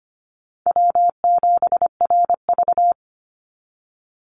Morse code, heard as a keyed tone, about 25 wpm, 700 Hz. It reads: W7RV
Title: NA SOTA Chaser Callsigns 25wpm